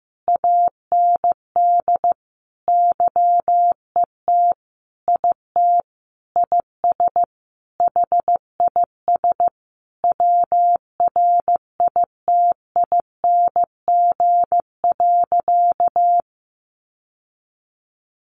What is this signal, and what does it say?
Morse code, heard as a keyed tone, about 15 wpm, 700 Hz. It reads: AND YET IT IS HIS WRITING.
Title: Sentences from Top 800 Words 15wpm